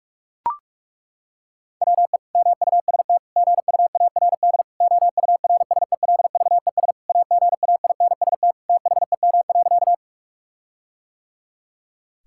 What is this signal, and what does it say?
Morse code, heard as a keyed tone, about 41 wpm, 700 Hz. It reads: WE MUST GUARD OURSELVES AGAINST THEM.
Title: Sentences from Top 1600 Words 45wpm